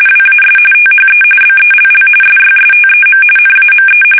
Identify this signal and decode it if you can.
RTTY, about 45.45 baud. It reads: WAS THE BELL SYSTEM NAME
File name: fsk850 / teletype-corp